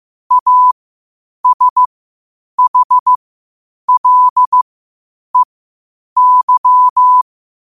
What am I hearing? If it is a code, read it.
Morse code, heard as a keyed tone, about 15 wpm, spaced as for 8 wpm, 1000 Hz. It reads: ASHLEY